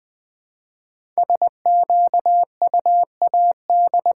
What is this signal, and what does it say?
Morse code, heard as a keyed tone, about 20 wpm, 700 Hz. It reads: SQUAD
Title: 5 Letter Words - First 3 Letters Given - 20wpm